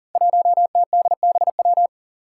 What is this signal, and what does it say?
Morse code, heard as a keyed tone, about 42 wpm, 700 Hz. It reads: 1TDBW
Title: Five Challenging Letter-Number Combinations 40wpm